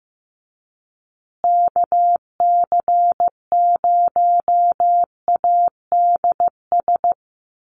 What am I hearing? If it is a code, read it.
Morse code, heard as a keyed tone, about 15 wpm, 700 Hz. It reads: KC0ADS
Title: USA Call Signs Repeated 2x 15wpm